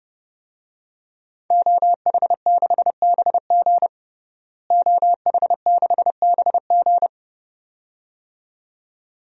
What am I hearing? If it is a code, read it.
Morse code, heard as a keyed tone, about 30 wpm, 700 Hz. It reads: OH6BG OH6BG
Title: DXCC-100 Call Signs - 5 Characters - Repeated 2x - 30wpm